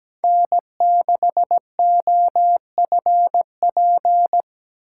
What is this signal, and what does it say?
Morse code, heard as a keyed tone, about 17 wpm, 700 Hz. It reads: N6OFP
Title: USA Call Signs - 5 Characters - 8x Character Spacing - 17wpm